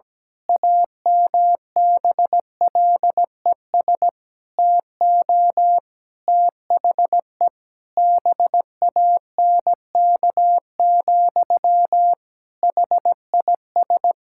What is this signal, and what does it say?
Morse code, heard as a keyed tone, about 17 wpm, 700 Hz. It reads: AMBLES TO THE BANK, HIS
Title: Pangram Challenge v2 - 17wpm